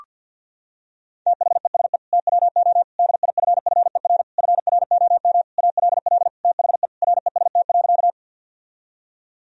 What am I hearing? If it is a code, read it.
Morse code, heard as a keyed tone, about 52 wpm, 700 Hz. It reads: THESE TWO DIFFER FROM ALL THE REST.
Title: Sentences from Top 1000 Words - Review Part 1 - 50wpm